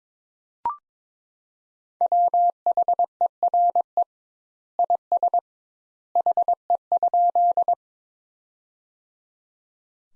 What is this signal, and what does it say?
Morse code, heard as a keyed tone, about 22 wpm, 700 Hz. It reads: WHERE IS HE?
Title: Sentences Top 200 Words - 22wpm